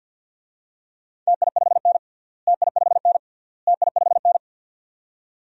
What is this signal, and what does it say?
Morse code, heard as a keyed tone, about 52 wpm, 700 Hz. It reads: TI5N TI5N TI5N
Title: DXCC-100 Call Signs - 4 Characters - Repeated 3x - 50wpm